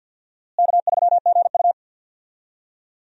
Morse code, heard as a keyed tone, about 52 wpm, 700 Hz. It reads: K3GU
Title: USA Call Signs - 4 Characters - Repeated 3x - 50wpm